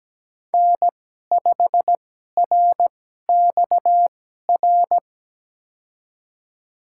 Morse code, heard as a keyed tone, about 17 wpm, spaced as for 12 wpm, 700 Hz. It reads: N5RXR
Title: USA Call Signs - 5 Characters - 2x Character Spacing - 17wpm